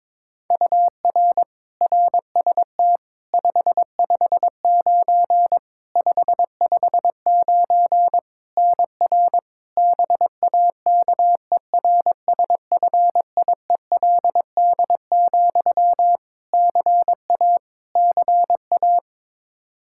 Morse code, heard as a keyed tone, about 22 wpm, 700 Hz. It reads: UR RST 559 559 NR BAKERSFIELD, CA CA
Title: Practice QSO - Exchange 1 - Beginning - 22wpm